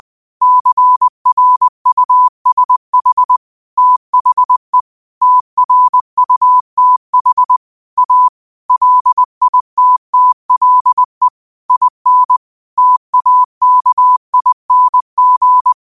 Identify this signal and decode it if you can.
Morse code, heard as a keyed tone, about 20 wpm, 1000 Hz. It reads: CRUSH THE TRUTH A LITTLE IN TAKING
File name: qotdfiles / 20wpm